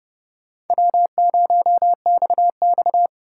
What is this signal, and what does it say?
Morse code, heard as a keyed tone, about 32 wpm, 700 Hz. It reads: W0XX